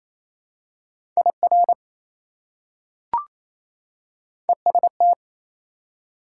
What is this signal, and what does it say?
Morse code, heard as a keyed tone, about 28 wpm, 700 Hz. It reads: IR EST
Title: Prefixes and Suffixes 28wpm